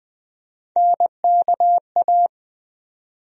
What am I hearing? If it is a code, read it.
Morse code, heard as a keyed tone, about 20 wpm, 700 Hz. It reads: NKA
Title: USA Call Sign Postfixes - 3 Letters 20wpm